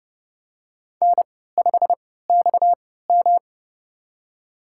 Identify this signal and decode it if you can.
Morse code, heard as a keyed tone, about 30 wpm, spaced as for 17 wpm, 700 Hz. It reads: N5XM
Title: USA Call Signs - 4 Characters - 3x Character Spacing - 30wpm